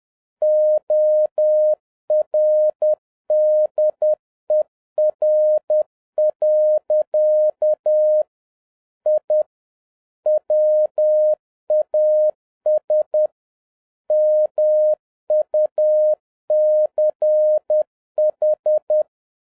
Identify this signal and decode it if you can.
Morse code, heard as a keyed tone, about 10 wpm, 610 Hz. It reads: ORDER. I WAS MUCH